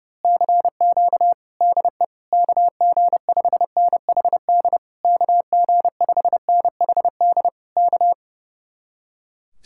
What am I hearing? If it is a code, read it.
Morse code, heard as a keyed tone, about 28 wpm, 700 Hz. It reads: CQ DE KG5NHD KG5NHD K
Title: Speed-Racing Common QSO Phrases Encore - 30wpm